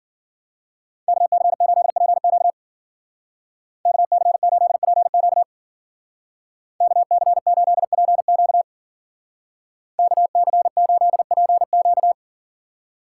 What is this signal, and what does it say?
Morse code, heard as a keyed tone, about 49 wpm, 700 Hz. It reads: KC8PQ KC8PQ KC8PQ KC8PQ
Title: Speed Racing - USA Call Signs 40wpm